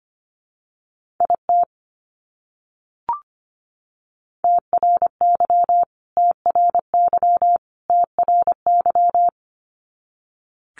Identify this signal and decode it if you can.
Morse code, heard as a keyed tone, about 25 wpm, 700 Hz. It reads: IT TRY TRY TRY
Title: Top 200 Words Repeated 25wpm